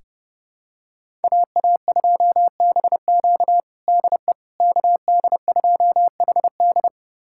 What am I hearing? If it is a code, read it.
Morse code, heard as a keyed tone, about 31 wpm, 700 Hz. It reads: AA2BQ DE KD2HD
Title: USA Callsign DE Callsign - 5 Characters - 30wpm